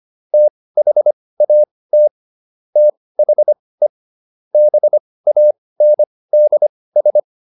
Morse code, heard as a keyed tone, about 25 wpm, spaced as for 18 wpm, 600 Hz. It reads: THAT THE BANDS